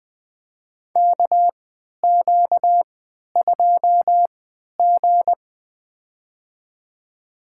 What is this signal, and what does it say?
Morse code, heard as a keyed tone, about 20 wpm, spaced as for 11 wpm, 700 Hz. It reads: KQ2G
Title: USA Call Signs - 4 Characters - 3x Character Spacing - 20wpm